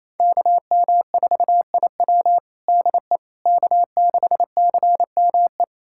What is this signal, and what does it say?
Morse code, heard as a keyed tone, about 28 wpm, 700 Hz. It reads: KM4IW DE K6CME
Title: USA Callsign DE Callsign - 5 Characters - 28wpm